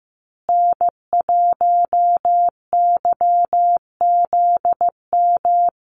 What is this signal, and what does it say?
Morse code, heard as a keyed tone, about 15 wpm, 700 Hz. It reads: N1YZM
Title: USA Call Signs Repeated 3x 15wpm